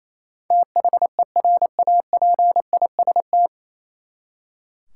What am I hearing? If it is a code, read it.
Morse code, heard as a keyed tone, about 28 wpm, 700 Hz. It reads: THERAPIST
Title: Practice QSO - Exchange 3 - More Info - 28wpm